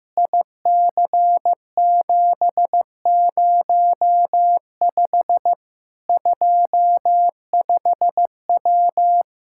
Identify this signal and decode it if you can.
Morse code, heard as a keyed tone, about 15 wpm, 700 Hz. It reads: IC705 25W
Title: K1USN SST QSO Practice 15wpm